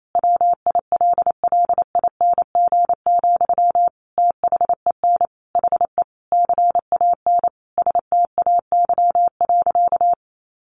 Morse code, heard as a keyed tone, about 28 wpm, 700 Hz. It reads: WILLING, THEN HE CAN STAY.
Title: Sentences from Top 1000 Words 28wpm